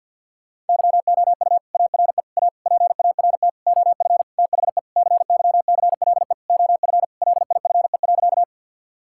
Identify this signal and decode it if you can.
Morse code, heard as a keyed tone, about 52 wpm, 700 Hz. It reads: YOU ARE A PART OF THE CYCLE OF LIFE.
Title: Sentences from Top 1000 Words - Review Part 6 - 50wpm